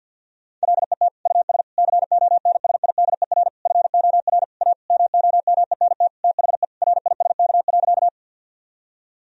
Morse code, heard as a keyed tone, about 50 wpm, 700 Hz. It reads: LET US CONSIDER FOR A MOMENT THE RISK.
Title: Sentences from Top 1000 Words - Review Part 4 - 50wpm